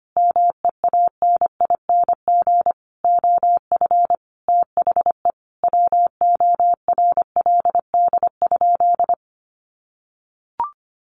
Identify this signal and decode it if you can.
Morse code, heard as a keyed tone, about 25 wpm, 700 Hz. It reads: MEANING OF THE WORLD?
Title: Sentences from Top 900 Words 25wpm